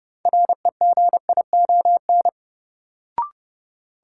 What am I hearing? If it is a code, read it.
Morse code, heard as a keyed tone, about 31 wpm, 700 Hz. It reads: REGION
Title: Sentences from Top 1500 Words - New Word Drill - 30wpm